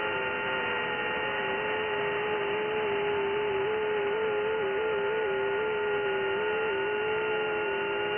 FT8.
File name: IC-7100